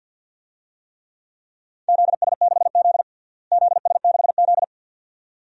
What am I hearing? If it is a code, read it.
Morse code, heard as a keyed tone, about 48 wpm, 700 Hz. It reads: ZS6Z ZS6Z
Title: DXCC-100 Call Signs - 4 Characters - Repeated 2x - 50wpm